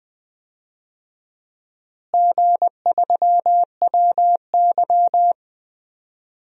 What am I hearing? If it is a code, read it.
Morse code, heard as a keyed tone, about 21 wpm, 700 Hz. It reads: G3WY
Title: DXCC-100 Call Signs - 4 Characters - Repeated 3x - 20wpm